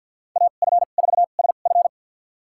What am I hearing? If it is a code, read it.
Morse code, heard as a keyed tone, about 53 wpm, spaced as for 37 wpm, 690 Hz. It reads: AF4SF
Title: USA Call Signs - 2x Character Spacing - 50wpm